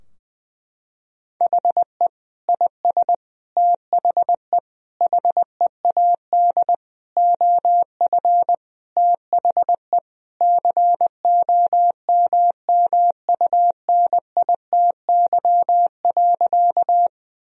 Morse code, heard as a keyed tone, about 21 wpm, 700 Hz. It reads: HE IS THE HEAD OF THE COMMUNITY.
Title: Sentences from Top 1000 Words - Review Part 4 - 20wpm